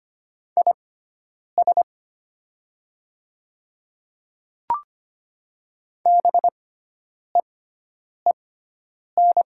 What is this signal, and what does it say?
Morse code, heard as a keyed tone, about 25 wpm, spaced as for 9 wpm, 700 Hz. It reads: IS BEEN
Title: 6x Character Spacing - Top 300 Words - 25wpm